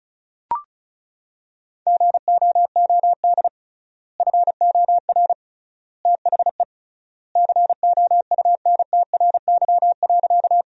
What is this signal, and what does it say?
Morse code, heard as a keyed tone, about 36 wpm, 700 Hz. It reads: GOOD FOR THE COUNTRY.
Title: 3x Word-Spacing - Sentences from Top 300 Words - 35wpm